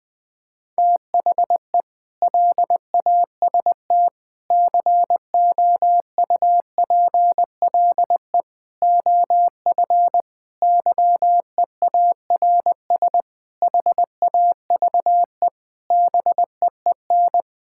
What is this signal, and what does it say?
Morse code, heard as a keyed tone, about 20 wpm, 700 Hz. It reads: THE LAST COUPLE OF YEARS HAVE BEEN
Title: Sentences with 5-Word Phrases - 20wpm